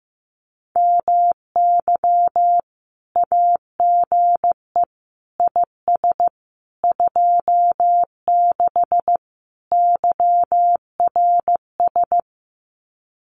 Morse code, heard as a keyed tone, about 15 wpm, 700 Hz. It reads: MY AGE IS 26 YRS
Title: Common QSO Phrases Encore 15wpm